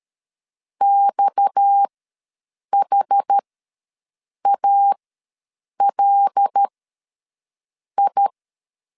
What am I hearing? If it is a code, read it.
Morse code, heard as a keyed tone, about 13 wpm, spaced as for 6 wpm, 800 Hz. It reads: XHALI
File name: LEZIONI / 8 YJ45 / 8 YJ4 ws 12wpm 3249